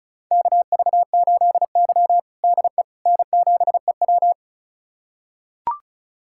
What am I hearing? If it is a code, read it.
Morse code, heard as a keyed tone, about 36 wpm, 700 Hz. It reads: KV8Y DE N7EW
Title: USA Callsign DE Callsign - 4 Characters - 35wpm